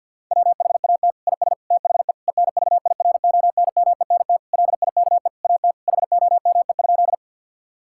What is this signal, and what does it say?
Morse code, heard as a keyed tone, about 50 wpm, 700 Hz. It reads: WHAT IS THE ENVIRONMENT LIKE AT HOME?
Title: Sentences from Top 1000 Words - Review Part 5 - 50wpm